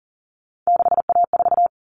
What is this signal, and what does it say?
Morse code, heard as a keyed tone, about 40 wpm, 700 Hz. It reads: 6A4
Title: All Cut Numbers - 3-Digit 40wpm